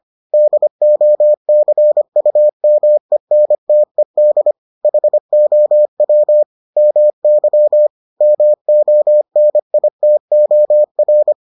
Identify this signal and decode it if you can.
Morse code, heard as a keyed tone, about 25 wpm, 600 Hz. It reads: DOCUMENTED HOW MY MONITOR